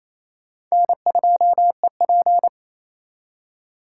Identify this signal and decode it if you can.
Morse code, heard as a keyed tone, about 37 wpm, 700 Hz. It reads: N2EP